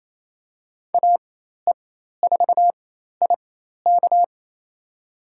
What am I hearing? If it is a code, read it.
Morse code, heard as a keyed tone, about 28 wpm, spaced as for 13 wpm, 700 Hz. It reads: AE4IK
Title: USA Call Signs - 5 Characters - 4x Character Spacing - 28wpm